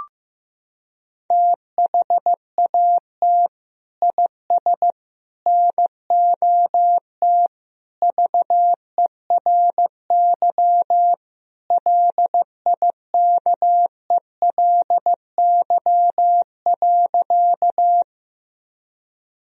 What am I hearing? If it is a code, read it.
Morse code, heard as a keyed tone, about 15 wpm, 700 Hz. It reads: THAT IS NOT VERY LIKELY.
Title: Sentences from Top 1000 Words - Review Part 4 - 15wpm